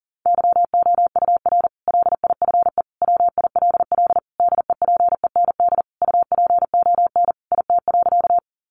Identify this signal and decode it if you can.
Morse code, heard as a keyed tone, about 40 wpm, 700 Hz. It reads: YOUR LIFE WILL DEPEND UPON IT.
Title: Sentences from Top 1300 Words 40wpm